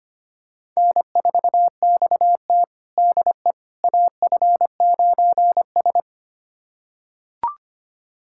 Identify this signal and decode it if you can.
Morse code, heard as a keyed tone, about 25 wpm, 700 Hz. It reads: N4XT DE AF9S
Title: USA Callsign DE Callsign - 4 Characters - 25wpm